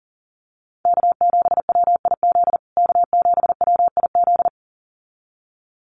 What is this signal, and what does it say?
Morse code, heard as a keyed tone, about 40 wpm, 700 Hz. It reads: K7WIZ K7WIZ
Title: USA Call Signs Repeated 2x 40wpm